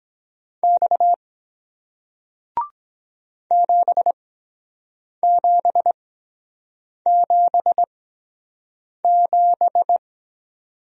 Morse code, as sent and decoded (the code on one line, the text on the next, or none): -..- / --... --... --... --...
X 7777